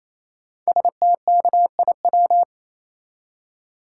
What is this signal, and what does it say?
Morse code, heard as a keyed tone, about 27 wpm, 700 Hz. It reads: STKIW
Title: Code Groups - 5 Letters - 28wpm